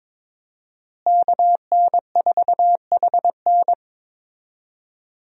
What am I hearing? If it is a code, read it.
Morse code, heard as a keyed tone, about 21 wpm, 700 Hz. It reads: KN4HN
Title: USA Call Signs - 5 Characters - Repeated 3x - 22wpm